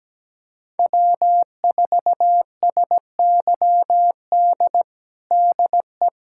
Morse code, heard as a keyed tone, about 17 wpm, 700 Hz. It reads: W4SYD DE
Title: USA Callsign DE Callsign - 5 Characters - 17wpm